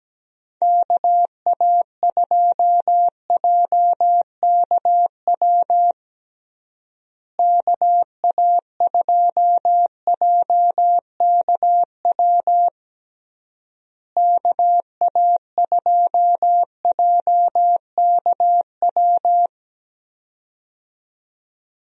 Morse code, heard as a keyed tone, about 17 wpm, 700 Hz. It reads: KA2JKW KA2JKW KA2JKW
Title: USA Call Signs - 6 Characters - Repeated 3x - 17wpm